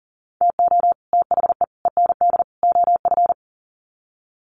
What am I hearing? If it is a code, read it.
Morse code, heard as a keyed tone, about 40 wpm, 700 Hz. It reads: TO THE END OF
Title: Sets of 4 Words 40wpm